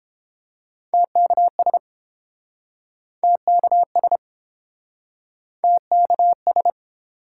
Morse code, heard as a keyed tone, about 30 wpm, 700 Hz. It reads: TKS TKS TKS